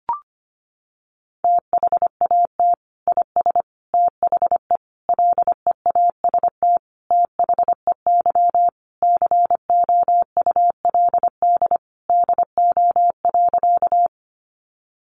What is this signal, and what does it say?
Morse code, heard as a keyed tone, about 25 wpm, 700 Hz. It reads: THAT IS THE LEAST THEY COULD DO.
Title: Sentences from Top 700 Words 25wpm